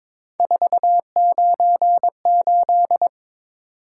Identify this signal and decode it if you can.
Morse code, heard as a keyed tone, about 22 wpm, 700 Hz. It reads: 498
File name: practice / 22wpm / Three Letter-Number Combinations - 22wpm